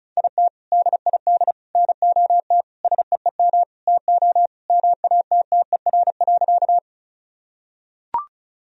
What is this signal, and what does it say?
Morse code, heard as a keyed tone, about 34 wpm, 700 Hz. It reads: IT DID NOT SEEM TO MATTER.
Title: Sentences from Top 1000 Words - Review Part 1 - 35wpm